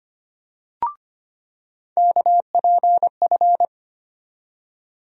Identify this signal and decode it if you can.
Morse code, heard as a keyed tone, about 25 wpm, 700 Hz. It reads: KPF
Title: USA Call Sign Postfixes - 3 Letters 25wpm